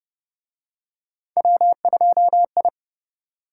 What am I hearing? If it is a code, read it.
Morse code, heard as a keyed tone, about 24 wpm, 700 Hz. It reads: W2I
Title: Speed-Racing - Three Letter-Number Combinations - 20wpm